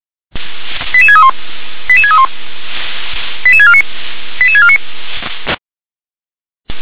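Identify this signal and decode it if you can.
Five-tone selective call (ZVEI), heard as quick runs of five tones, tones of about 70 ms. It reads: 88631 88631 88649 88649
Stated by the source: ZVEI five-tone selective call received via SDR.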